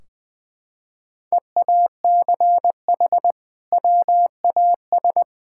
Morse code, heard as a keyed tone, about 20 wpm, 700 Hz. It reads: EACH WAS
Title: Sentences from Top 1600 Words 20wpm